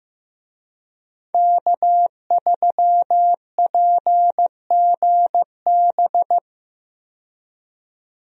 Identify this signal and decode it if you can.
Morse code, heard as a keyed tone, about 15 wpm, 700 Hz. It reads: K3PGB
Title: USA Call Signs - 5 Characters - 15wpm